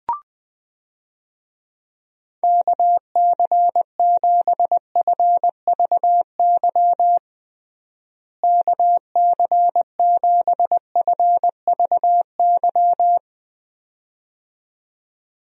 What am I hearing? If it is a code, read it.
Morse code, heard as a keyed tone, about 20 wpm, 700 Hz. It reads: KC7FVY KC7FVY
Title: USA Call Signs - 6 Characters - Repeated 2x - 20wpm